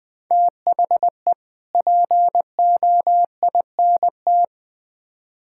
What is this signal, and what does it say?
Morse code, heard as a keyed tone, about 20 wpm, 700 Hz. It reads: THE POINT
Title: Build A Phrase - 5-Word Phrases - 20wpm